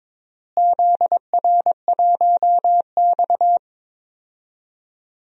Morse code, heard as a keyed tone, about 21 wpm, 700 Hz. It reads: ZR1X
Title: DXCC-100 Call Signs - 4 Characters - Repeated 2x - 22wpm